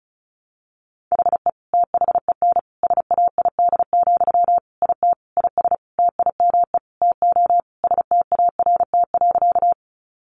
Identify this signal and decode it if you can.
Morse code, heard as a keyed tone, about 34 wpm, 700 Hz. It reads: HE THEN SAID, IT IS TIME TO START.
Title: Sentences from Top 300 Words 35wpm